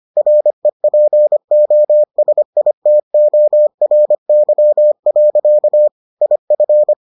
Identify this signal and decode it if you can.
Morse code, heard as a keyed tone, about 25 wpm, 600 Hz. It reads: REPOSITORY. IF